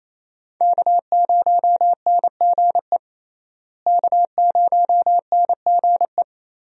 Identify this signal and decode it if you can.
Morse code, heard as a keyed tone, about 28 wpm, 700 Hz. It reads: K0NGE K0NGE
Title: USA Call Signs - 5 Characters - Repeated 2x - 28wpm